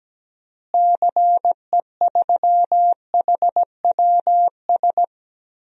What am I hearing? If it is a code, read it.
Morse code, heard as a keyed tone, about 17 wpm, 700 Hz. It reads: CE3HWS
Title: DXCC-100 Call Signs - 6 Characters - Repeated 3x - 17wpm